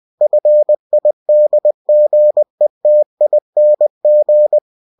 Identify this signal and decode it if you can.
Morse code, heard as a keyed tone, about 20 wpm, 600 Hz. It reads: FIDGETING
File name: cw-books / winnie-the-pooh / winnie-20wpm-page-0066